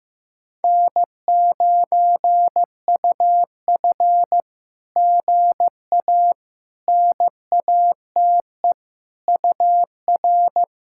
Morse code, heard as a keyed tone, about 15 wpm, 700 Hz. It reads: N9UF GA NATE UR
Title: SOTA QSO Practice 15wpm